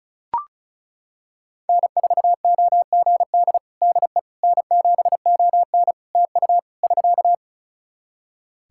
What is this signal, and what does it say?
Morse code, heard as a keyed tone, about 34 wpm, 700 Hz. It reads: N4OGD DE N7ON TU <SK>
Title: Practice QSO - Exchange 4 - Ending - 35wpm